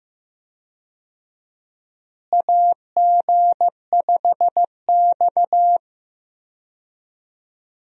Morse code, heard as a keyed tone, about 15 wpm, 700 Hz. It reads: AG5X
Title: USA Call Signs - 4 Characters - Repeated 3x - 15wpm